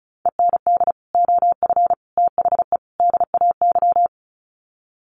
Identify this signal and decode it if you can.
Morse code, heard as a keyed tone, about 34 wpm, 700 Hz. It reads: END OF THE DAY
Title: Sets of 4 Words 35wpm